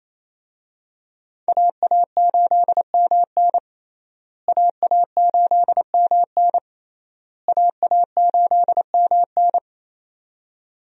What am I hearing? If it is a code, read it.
Morse code, heard as a keyed tone, about 28 wpm, 700 Hz. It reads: AA8MN AA8MN AA8MN
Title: USA Call Signs - 5 Characters - Repeated 3x - 28wpm